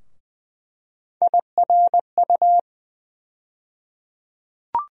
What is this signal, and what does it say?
Morse code, heard as a keyed tone, about 20 wpm, 700 Hz. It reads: IRU
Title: USA Call Sign Postfixes - 3 Letters 20wpm